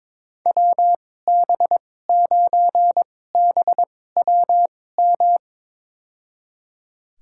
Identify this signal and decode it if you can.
Morse code, heard as a keyed tone, about 22 wpm, spaced as for 16 wpm, 700 Hz. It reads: WB9BWM
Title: USA Call Signs - 6 Characters - 2x Character Spacing - 22wpm